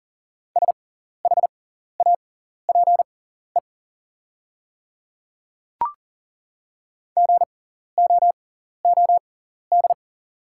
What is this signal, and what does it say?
Morse code, heard as a keyed tone, about 40 wpm, spaced as for 14 wpm, 700 Hz. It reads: SHAPE GOOD
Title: 6x Character Spacing - Top 500 Words - 40wpm